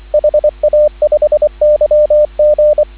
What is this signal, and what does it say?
Morse code, heard as a keyed tone, about 24 wpm, 590 Hz, with radio noise. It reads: HA5YG
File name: CW / 2017 HA-OB HA5YG